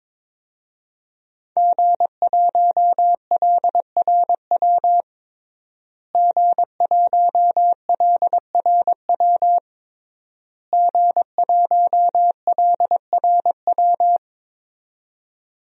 Morse code, heard as a keyed tone, about 21 wpm, 700 Hz. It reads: G1LRW G1LRW G1LRW
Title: DXCC-100 Call Signs - 5 Characters - Repeated 3x - 22wpm